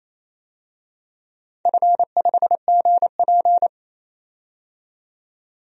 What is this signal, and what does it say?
Morse code, heard as a keyed tone, about 28 wpm, 700 Hz. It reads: F5GP
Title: DXCC-100 Call Signs - 4 Characters - 28wpm